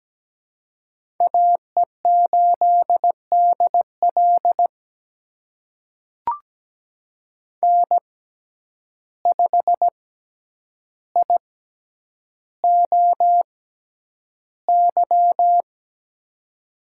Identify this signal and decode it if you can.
Morse code, heard as a keyed tone, about 17 wpm, spaced as for 9 wpm, 700 Hz. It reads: AE8DL N5IOY
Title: USA Call Signs - 5 Characters - 6x Character Spacing - 17wpm